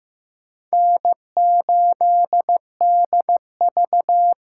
Morse code, heard as a keyed tone, about 18 wpm, 700 Hz. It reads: N8DV
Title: Speed Racing - USA 1x2 Call Signs 15wpm